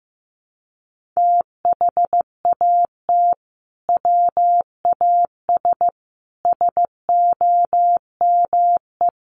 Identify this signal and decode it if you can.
Morse code, heard as a keyed tone, about 14 wpm, 700 Hz. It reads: THAT WAS SOME
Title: Sentences from Top 1400 Words 15wpm